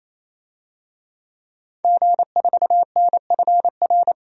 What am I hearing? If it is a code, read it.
Morse code, heard as a keyed tone, about 28 wpm, 700 Hz. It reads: G4NFR
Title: DXCC-100 Call Signs - 5 Characters - 28wpm